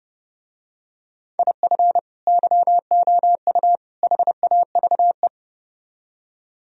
Morse code, heard as a keyed tone, about 30 wpm, 700 Hz. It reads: IF YOU HAVE